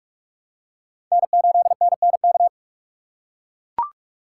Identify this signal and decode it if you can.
Morse code, heard as a keyed tone, about 43 wpm, 700 Hz. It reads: N8NNK
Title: USA Call Signs - 5 Characters - 6x Character Spacing - 45wpm